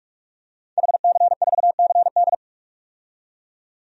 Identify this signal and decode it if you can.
Morse code, heard as a keyed tone, about 46 wpm, 700 Hz. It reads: HC4CD
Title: DXCC-100 Call Signs - 5 Characters - 45wpm